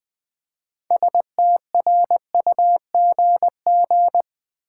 Morse code, heard as a keyed tone, about 20 wpm, 700 Hz. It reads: STRUGG
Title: Build A Word - 3 Letters At A Time - 8-Letter Words - 20wpm